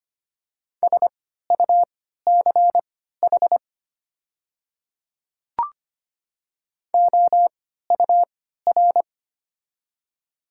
Morse code, heard as a keyed tone, about 25 wpm, spaced as for 14 wpm, 690 Hz. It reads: SUCH OUR
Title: 3x Character Spacing - Top 300 Words - 25wpm